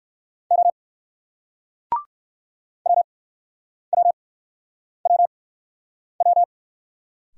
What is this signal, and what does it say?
Morse code, heard as a keyed tone, about 59 wpm, spaced as for 9 wpm, 700 Hz. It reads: O WWWW